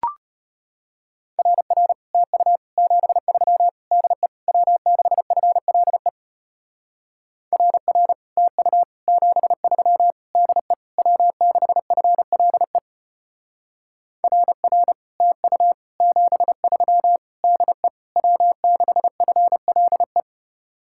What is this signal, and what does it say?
Morse code, heard as a keyed tone, about 30 wpm, 700 Hz. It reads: RR TU 73 DE W6FLE RR TU 73 DE W6FLE RR TU 73 DE W6FLE
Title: Speed-Racing Common QSO Phrases Encore - 25wpm